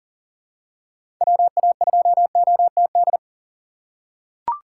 Morse code, heard as a keyed tone, about 40 wpm, 700 Hz. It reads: WA2OTD
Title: USA Call Signs - 6 Characters - 4x Character Spacing - 40wpm